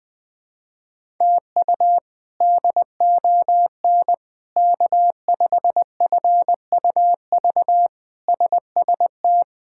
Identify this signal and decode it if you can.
Morse code, heard as a keyed tone, about 20 wpm, 700 Hz. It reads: TU DON K5FUV SST
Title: K1USN SST QSO Practice - 20wpm